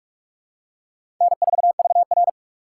Morse code, heard as a keyed tone, about 46 wpm, 700 Hz. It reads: N4VR
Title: USA Call Signs - 6x Character Spacing - 45wpm